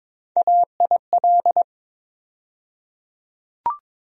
Morse code, heard as a keyed tone, about 22 wpm, 700 Hz. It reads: AIL
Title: USA Call Sign Postfixes - 3 Letters 22wpm